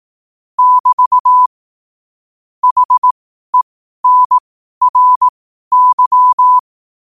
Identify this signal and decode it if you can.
Morse code, heard as a keyed tone, about 18 wpm, spaced as for 12 wpm, 1000 Hz. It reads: <BT> HENRY
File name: qotdfiles / 12wpm